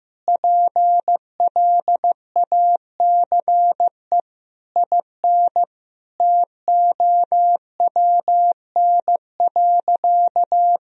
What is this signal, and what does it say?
Morse code, heard as a keyed tone, about 15 wpm, 700 Hz. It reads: PLACE IN TOWN.
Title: Sentences from Top 1000 Words - Review Part 2 - 15wpm